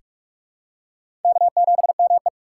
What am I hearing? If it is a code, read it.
Morse code, heard as a keyed tone, about 44 wpm, 700 Hz. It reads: K7ME
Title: USA Call Signs - 4 Characters - 6x Character Spacing - 45wpm